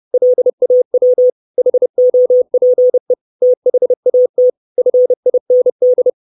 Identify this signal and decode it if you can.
Morse code, heard as a keyed tone, about 30 wpm, 500 Hz. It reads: LAW HOPE THAT FIND
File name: 632 Words 30WPM